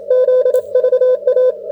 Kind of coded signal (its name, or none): Morse code